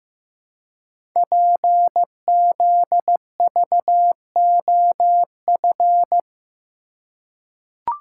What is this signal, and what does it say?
Morse code, heard as a keyed tone, about 15 wpm, 700 Hz. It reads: PZVOF
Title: Code Groups - 5 Letters - 15wpm